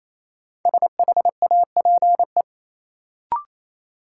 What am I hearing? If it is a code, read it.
Morse code, heard as a keyed tone, about 28 wpm, 700 Hz. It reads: SHAPE
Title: Sentences from Top 500 Words - New Word Drill - 28wpm